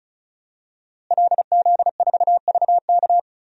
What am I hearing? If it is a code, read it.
Morse code, heard as a keyed tone, about 34 wpm, 700 Hz. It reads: LZ4VK
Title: DXCC-100 Call Signs - 5 Characters - Repeated 3x - 35wpm